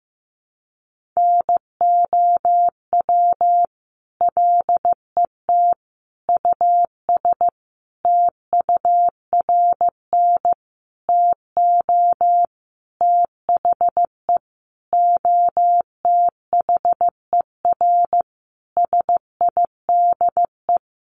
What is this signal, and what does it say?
Morse code, heard as a keyed tone, about 15 wpm, 700 Hz. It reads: NOW LET US TURN TO THE OTHER SIDE
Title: Sentences from Top 1400 Words 15wpm